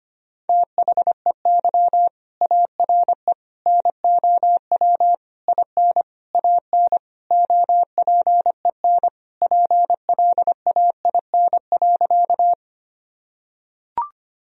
Morse code, heard as a keyed tone, about 25 wpm, 700 Hz. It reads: THEY ARE NOW IN AN OPEN PLAIN.
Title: Sentences from Top 1000 Words - Review Part 2 - 25wpm